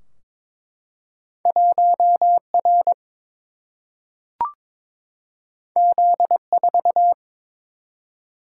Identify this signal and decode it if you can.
Morse code, heard as a keyed tone, about 22 wpm, 700 Hz. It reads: 1 R Z 4
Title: Single-Letter Single-Number 22wpm